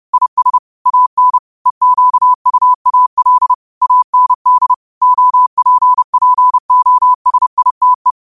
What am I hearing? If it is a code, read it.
Morse code, heard as a keyed tone, about 30 wpm, 1000 Hz. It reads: IS AN EQUAL AND OPPOSITE
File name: qotdfiles / 30wpm